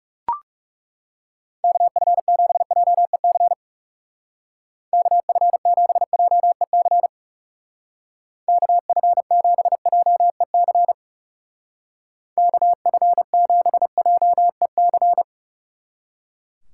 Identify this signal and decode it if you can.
Morse code, heard as a keyed tone, about 35 wpm, 700 Hz. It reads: KF7JEC KF7JEC KF7JEC KF7JEC
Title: Speed Racing - USA 2x3 Call Signs 30wpm